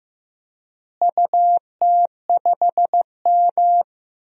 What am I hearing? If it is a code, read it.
Morse code, heard as a keyed tone, about 15 wpm, 700 Hz. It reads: UT5M
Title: DXCC-100 Call Signs - 4 Characters - Repeated 3x - 15wpm